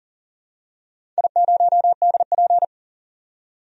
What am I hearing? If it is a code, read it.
Morse code, heard as a keyed tone, about 41 wpm, 700 Hz. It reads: I0DP
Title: DXCC-100 Call Signs - 4 Characters - 40wpm